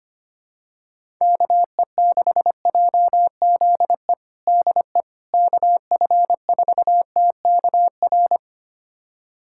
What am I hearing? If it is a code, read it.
Morse code, heard as a keyed tone, about 25 wpm, 700 Hz. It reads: KE6JZE DE KF4TKR
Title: USA Callsign DE Callsign - 6 Characters - 25wpm